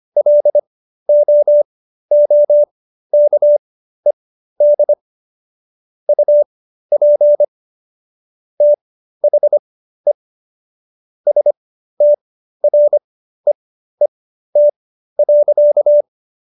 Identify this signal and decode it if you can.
Morse code, heard as a keyed tone, about 25 wpm, spaced as for 13 wpm, 590 Hz. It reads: LOOKED UP THE STREET.